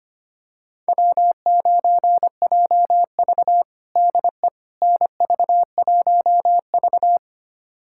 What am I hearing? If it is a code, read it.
Morse code, heard as a keyed tone, about 25 wpm, 700 Hz. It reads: W9JV DE NV1V
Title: USA Callsign DE Callsign - 4 Characters - 25wpm